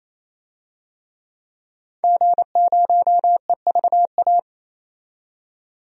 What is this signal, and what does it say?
Morse code, heard as a keyed tone, about 27 wpm, 700 Hz. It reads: G0EVA
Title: DXCC-100 Call Signs - 5 Characters - 28wpm